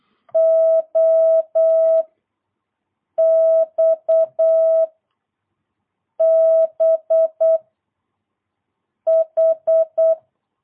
Morse code, heard as a keyed tone, about 8 wpm, 650 Hz. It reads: O X B H
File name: jh8cbh / 2022cw